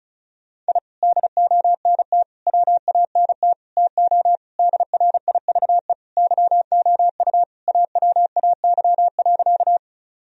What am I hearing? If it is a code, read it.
Morse code, heard as a keyed tone, about 36 wpm, 700 Hz. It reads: I DONT WANT TO DRIVE YOU AWAY.
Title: Sentences from Top 1000 Words - Review Part 2 - 35wpm